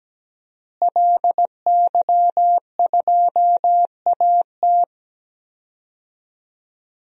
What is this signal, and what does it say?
Morse code, heard as a keyed tone, about 17 wpm, 700 Hz. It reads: LY2AT
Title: DXCC-100 Call Signs - 5 Characters - Repeated 2x - 17wpm